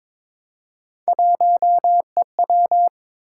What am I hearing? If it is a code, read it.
Morse code, heard as a keyed tone, about 22 wpm, 700 Hz. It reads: 1EW
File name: practice / 22wpm / Three Letter-Number Combinations - 22wpm